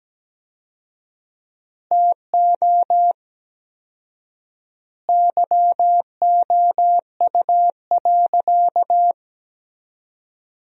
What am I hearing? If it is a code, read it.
Morse code, heard as a keyed tone, about 17 wpm, 700 Hz. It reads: TO YOU.
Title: 4x Word-Spacing - Sentences from Top 200 Words - 17wpm